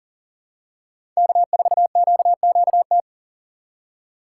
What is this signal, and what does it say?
Morse code, heard as a keyed tone, about 37 wpm, 700 Hz. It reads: K4QQT
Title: USA Call Signs - 5 Characters - 4x Character Spacing - 40wpm